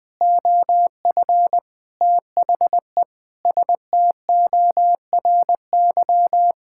Morse code, heard as a keyed tone, about 21 wpm, 700 Hz. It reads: OF THE STORY
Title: Sentences with 4-Word Phrases - 20wpm